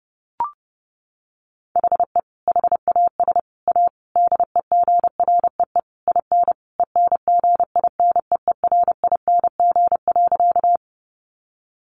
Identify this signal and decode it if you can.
Morse code, heard as a keyed tone, about 30 wpm, 700 Hz. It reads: HE HAS A DEGREE IN ENGINEERING.
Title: Sentences from Top 1500 Words 30wpm